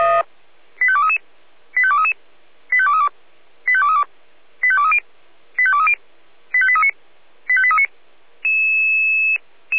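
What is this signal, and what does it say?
Five-tone selective call (ZVEI), heard as quick runs of five tones, tones of about 70 ms. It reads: 85233 85233 85232 85232 85230 85230 85839 85839
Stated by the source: ZVEI five-tone selective call received with an SDR.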